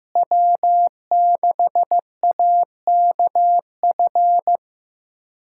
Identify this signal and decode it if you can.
Morse code, heard as a keyed tone, about 15 wpm, 700 Hz. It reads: W6AKF
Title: USA Callsign DE Callsign - 5 Characters - 15wpm